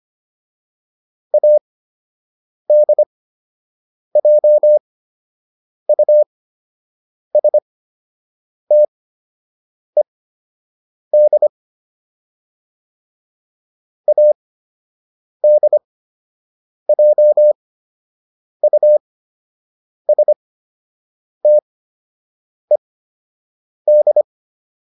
Morse code, heard as a keyed tone, about 25 wpm, spaced as for 7 wpm, 600 Hz. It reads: ADJUSTED ADJUSTED